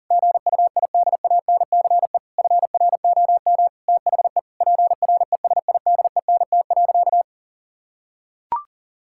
Morse code, heard as a keyed tone, about 40 wpm, 700 Hz. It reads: GUIDANCE FROM THE PRESIDENT.
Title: Sentences from Top 1600 Words 40wpm